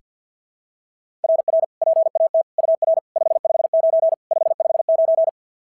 Morse code, heard as a keyed tone, about 46 wpm, 660 Hz. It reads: RR PAT UR 559 559
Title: SOTA QSO Practice 50wpm